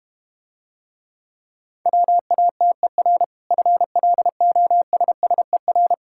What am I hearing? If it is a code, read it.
Morse code, heard as a keyed tone, about 34 wpm, 700 Hz. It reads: WATER FLOSSER